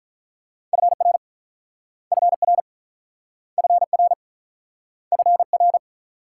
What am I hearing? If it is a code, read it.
Morse code, heard as a keyed tone, about 43 wpm, 700 Hz. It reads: FR FR FR FR
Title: Speed-Racing Multi-letter Phonemes - 35wpm